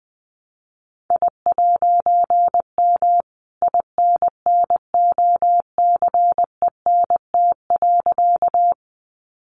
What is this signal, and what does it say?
Morse code, heard as a keyed tone, about 19 wpm, 700 Hz. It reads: I'M INNOCENT.
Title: A Sound Of Thunder 20wpm